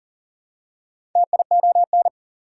Morse code, heard as a keyed tone, about 40 wpm, 700 Hz. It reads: TION
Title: Prefixes and Suffixes 40wpm